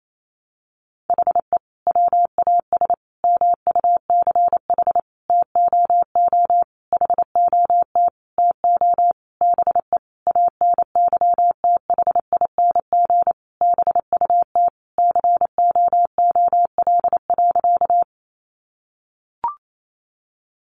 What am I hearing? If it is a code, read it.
Morse code, heard as a keyed tone, about 28 wpm, 700 Hz. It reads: HE WAS MUCH TOO HOT TO BE ANYTHING BUT COOL.
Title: Sentences from Top 1200 Words 28wpm